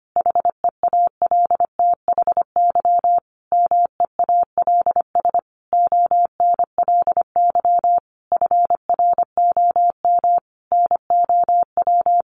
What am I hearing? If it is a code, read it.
Morse code, heard as a keyed tone, about 25 wpm, 700 Hz. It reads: HEALTHY MEALS ONLY FROM NOW
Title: Easy Phrases 25wpm